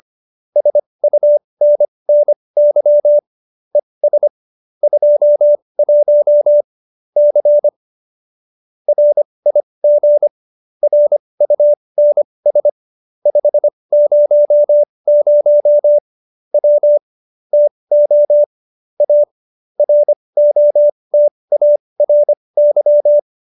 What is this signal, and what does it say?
Morse code, heard as a keyed tone, about 25 wpm, spaced as for 20 wpm, 590 Hz. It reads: SUNNY ES 21 C RIG RUNS 500 W TO A ROTARY